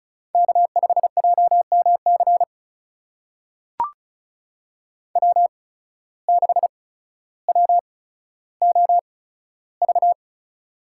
Morse code, heard as a keyed tone, about 36 wpm, spaced as for 15 wpm, 700 Hz. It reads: K5JMC W6WOV
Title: USA Call Signs - 5 Characters - 8x Character Spacing - 35wpm